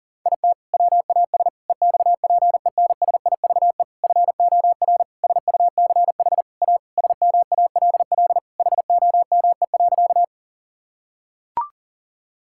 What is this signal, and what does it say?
Morse code, heard as a keyed tone, about 40 wpm, 700 Hz. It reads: IT WAS EXPENSIVE FOR SUCH A SMALL HOME.
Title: Sentences from Top 1000 Words - Review Part 6 - 40wpm